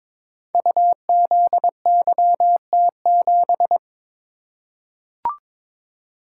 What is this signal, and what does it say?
Morse code, heard as a keyed tone, about 22 wpm, 700 Hz. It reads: UZYT7
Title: Code Groups - 5 Letter:Numbers - 22wpm